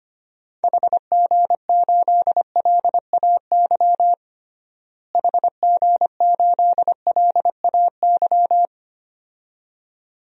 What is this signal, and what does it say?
Morse code, heard as a keyed tone, about 25 wpm, 700 Hz. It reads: HG8LAY HG8LAY
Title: DXCC-100 Call Signs - 6 Characters - Repeated 2x - 25wpm